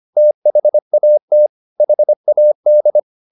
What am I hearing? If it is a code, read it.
Morse code, heard as a keyed tone, about 25 wpm, 600 Hz. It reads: THAT HAD